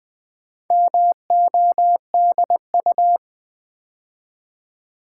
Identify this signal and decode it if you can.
Morse code, heard as a keyed tone, about 20 wpm, 700 Hz. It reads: MODU
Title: Build A Word - 2 Letters At A Time - 6-Letter Words - 20wpm